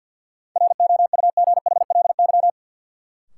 Morse code, heard as a keyed tone, about 51 wpm, 700 Hz. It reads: ROUGHLY
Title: Sentences from Top 1500 Words - New Word Drill Alt - 50wpm